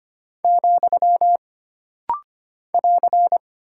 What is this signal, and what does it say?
Morse code, heard as a keyed tone, about 25 wpm, 700 Hz. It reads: , <AR>
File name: Single Punctuation-Prosign - Rapid-Fire - 25wpm